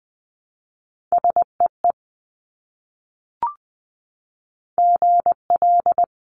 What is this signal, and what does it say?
Morse code, heard as a keyed tone, about 20 wpm, 700 Hz. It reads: SEE GL
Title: Multi-letter Phonemes Repeated 3x 20wpm